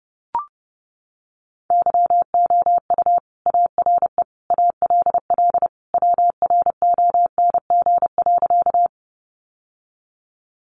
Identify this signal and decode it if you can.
Morse code, heard as a keyed tone, about 30 wpm, 700 Hz. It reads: YOU ARE ALL WRONG.
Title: Sentences from Top 900 Words 30wpm